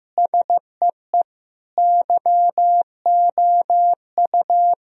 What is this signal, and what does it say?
Morse code, heard as a keyed tone, about 15 wpm, 700 Hz. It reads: SEE YOU
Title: Speed-Racing - Sets of 4 Words 15wpm